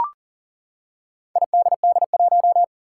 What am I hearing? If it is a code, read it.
Morse code, heard as a keyed tone, about 40 wpm, 700 Hz. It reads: IDD1
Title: Four Challenging Letter-Number Combinations 40wpm